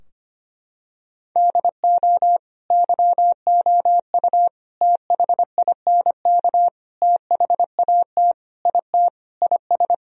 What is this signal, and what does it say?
Morse code, heard as a keyed tone, about 25 wpm, 700 Hz. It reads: DO YOU THINK THAT IT IS
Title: Sentences from Top 1000 Words - Review Part 4 - 25wpm